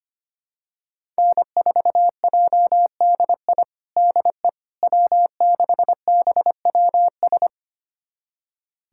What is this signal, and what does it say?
Morse code, heard as a keyed tone, about 25 wpm, 700 Hz. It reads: N4JDI DE W6BWS
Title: USA Callsign DE Callsign - 5 Characters - 25wpm